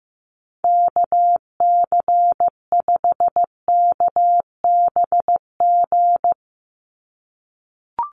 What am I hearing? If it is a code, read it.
Morse code, heard as a keyed tone, about 15 wpm, 700 Hz. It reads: KC5KBG
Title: USA Call Signs 15wpm